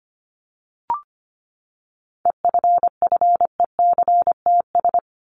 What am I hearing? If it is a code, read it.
Morse code, heard as a keyed tone, about 25 wpm, 700 Hz. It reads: EFFECTS
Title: ES:S Suffix - Top 100 Words with Base Words 25wpm